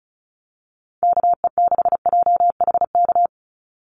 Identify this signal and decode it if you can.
Morse code, heard as a keyed tone, about 34 wpm, 700 Hz. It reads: KE6JHK
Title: USA Call Signs 35wpm